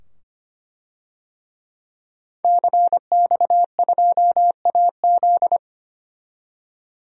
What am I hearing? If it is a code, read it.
Morse code, heard as a keyed tone, about 25 wpm, 700 Hz. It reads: CX2AZ
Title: DXCC-100 Call Signs - 5 Characters - 25wpm